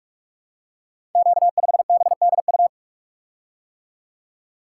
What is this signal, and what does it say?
Morse code, heard as a keyed tone, about 44 wpm, 700 Hz. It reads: Q5BDU
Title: Code Groups - 5 Letter:Numbers - 45wpm